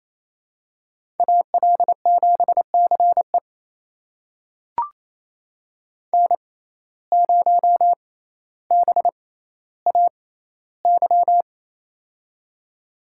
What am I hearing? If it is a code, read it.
Morse code, heard as a keyed tone, about 28 wpm, spaced as for 14 wpm, 700 Hz. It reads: AL7CE N0BAY
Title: USA Call Signs - 5 Characters - 6x Character Spacing - 28wpm